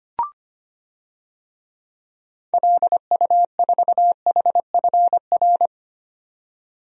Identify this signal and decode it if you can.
Morse code, heard as a keyed tone, about 25 wpm, 690 Hz. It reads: LU4HFR
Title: DXCC-100 Call Signs - 6 Characters - 25wpm